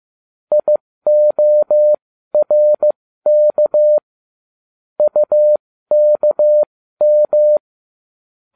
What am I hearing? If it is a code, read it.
Morse code, heard as a keyed tone, about 15 wpm, spaced as for 13 wpm, 600 Hz. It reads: IORK UKM